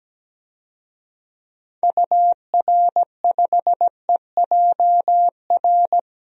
Morse code, heard as a keyed tone, about 17 wpm, 700 Hz. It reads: UR5EJR
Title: DXCC-100 Call Signs - 6 Characters - 17wpm